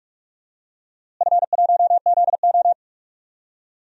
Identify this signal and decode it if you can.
Morse code, heard as a keyed tone, about 43 wpm, 700 Hz. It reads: F1ZO
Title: DXCC-100 Call Signs - 4 Characters - Repeated 2x - 45wpm